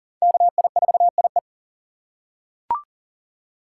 Morse code, heard as a keyed tone, about 40 wpm, 700 Hz. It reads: KI4IE
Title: USA Call Signs - 5 Characters - 4x Character Spacing - 40wpm